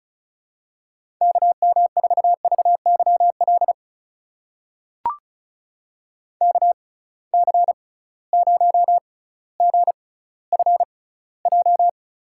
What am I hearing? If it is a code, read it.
Morse code, heard as a keyed tone, about 36 wpm, spaced as for 18 wpm, 700 Hz. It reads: KM4VYL KC0GFJ
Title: USA Call Signs - 6 Characters - 6x Character Spacing - 35wpm